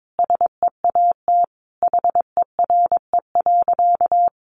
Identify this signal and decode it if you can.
Morse code, heard as a keyed tone, about 22 wpm, 700 Hz. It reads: SEAT HERE.
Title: Sentences from Top 1500 Words 22wpm